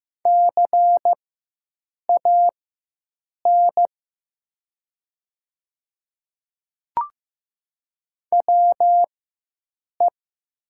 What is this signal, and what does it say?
Morse code, heard as a keyed tone, about 15 wpm, spaced as for 7 wpm, 700 Hz. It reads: CAN WE
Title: 4x Character Spacing - Top 100 Words - 15wpm